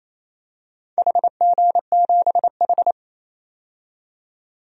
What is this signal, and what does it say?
Morse code, heard as a keyed tone, about 28 wpm, 700 Hz. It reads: HG7H
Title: DXCC-100 Call Signs - 4 Characters - 28wpm